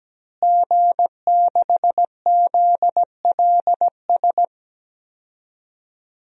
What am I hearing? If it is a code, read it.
Morse code, heard as a keyed tone, about 17 wpm, 700 Hz. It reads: G6ZLS
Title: DXCC-100 Call Signs - 5 Characters - Repeated 3x - 17wpm